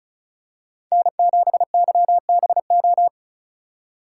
Morse code, heard as a keyed tone, about 36 wpm, 700 Hz. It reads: N7YBO
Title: USA Call Signs - 5 Characters - 4x Character Spacing - 35wpm